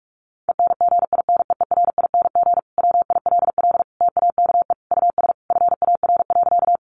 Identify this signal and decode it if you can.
Morse code, heard as a keyed tone, about 44 wpm, 700 Hz. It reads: ENGINEERING WILL TAKE US FAR.
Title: Sentences from Top 1500 Words 45wpm